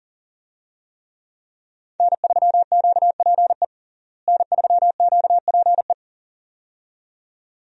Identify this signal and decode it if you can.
Morse code, heard as a keyed tone, about 40 wpm, 700 Hz. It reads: N3QPE N3QPE
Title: USA Call Signs - 5 Characters - Repeated 2x - 40wpm